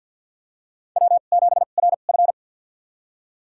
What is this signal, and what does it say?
Morse code, heard as a keyed tone, about 47 wpm, spaced as for 35 wpm, 700 Hz. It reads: W7RF
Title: USA Call Signs - 2x Character Spacing - 50wpm